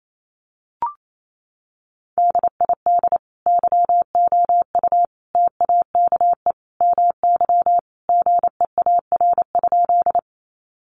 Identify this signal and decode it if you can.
Morse code, heard as a keyed tone, about 28 wpm, 700 Hz. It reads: DID YOU TAKE MY GEAR?
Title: Sentences from Top 1400 Words 28wpm